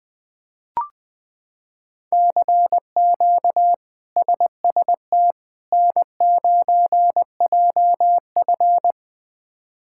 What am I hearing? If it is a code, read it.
Morse code, heard as a keyed tone, about 21 wpm, 700 Hz. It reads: CQ SST N9JF
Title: K1USN SST QSO Practice - 20wpm